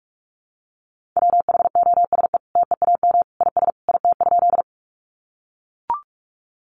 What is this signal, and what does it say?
Morse code, heard as a keyed tone, about 44 wpm, 700 Hz. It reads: WHOSE TEAM IS IT?
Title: Sentences from Top 1500 Words 45wpm